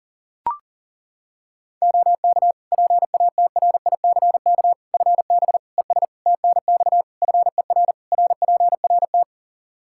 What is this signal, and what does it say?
Morse code, heard as a keyed tone, about 37 wpm, 700 Hz. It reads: OK PATRICK FB ES TNX FER RPRT
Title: Practice QSO - Exchange 2 - Info - 40wpm